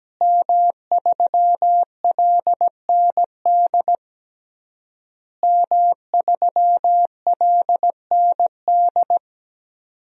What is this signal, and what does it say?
Morse code, heard as a keyed tone, about 17 wpm, 700 Hz. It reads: M3LND M3LND
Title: DXCC-100 Call Signs - 5 Characters - Repeated 3x - 17wpm